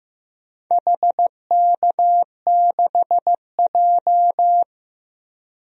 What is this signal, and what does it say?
Morse code, heard as a keyed tone, about 15 wpm, 700 Hz. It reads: HK6J
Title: DXCC-100 Call Signs - 4 Characters - 15wpm